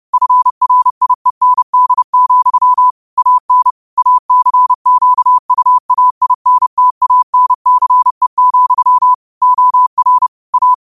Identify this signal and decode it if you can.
Morse code, heard as a keyed tone, about 30 wpm, 1000 Hz. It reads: FRIEND, AN ACQUAINTANCE, OR A
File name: qotdfiles / 30wpm